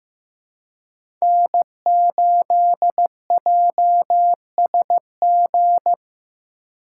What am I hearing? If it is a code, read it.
Morse code, heard as a keyed tone, about 15 wpm, 700 Hz. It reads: N8JSG
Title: USA Call Signs - 5 Characters - 3x Character Spacing - 15wpm